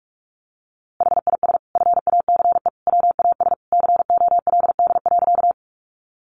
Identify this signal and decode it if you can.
Morse code, heard as a keyed tone, about 44 wpm, 700 Hz. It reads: HIS FACE WAS COLD.
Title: A Sound Of Thunder 45wpm